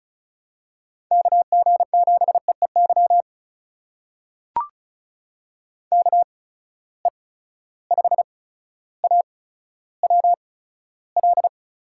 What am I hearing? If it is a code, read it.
Morse code, heard as a keyed tone, about 36 wpm, spaced as for 15 wpm, 700 Hz. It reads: KG7EEY KE5AWL
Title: USA Call Signs - 6 Characters - 8x Character Spacing - 35wpm